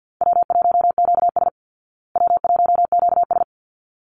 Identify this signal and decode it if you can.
Morse code, heard as a keyed tone, about 48 wpm, 700 Hz. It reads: W1QS W1QS
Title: Easy Call Signs Repeated 2x - 50wpm